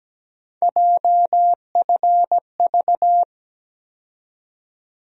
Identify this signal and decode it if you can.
Morse code, heard as a keyed tone, about 17 wpm, 700 Hz. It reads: JFV
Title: USA Call Sign Postfixes - 3 Letters 17wpm